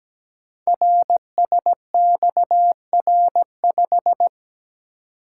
Morse code, heard as a keyed tone, about 17 wpm, 700 Hz. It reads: RSXR5
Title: Code Groups - 5 Letter:Numbers - 17wpm